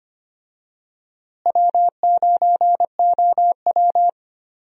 Morse code, heard as a keyed tone, about 25 wpm, 700 Hz. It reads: W9OW
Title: USA Call Signs - 4 Characters - Repeated 2x - 25wpm